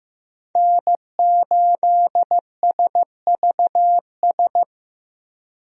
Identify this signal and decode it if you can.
Morse code, heard as a keyed tone, about 15 wpm, 700 Hz. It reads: N8SVS
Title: USA Call Signs - 5 Characters - Repeated 2x - 15wpm